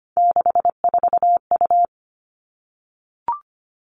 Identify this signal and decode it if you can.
Morse code, heard as a keyed tone, about 25 wpm, 700 Hz. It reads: 64U
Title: Three Challenging Letter-Number Combinations 25wpm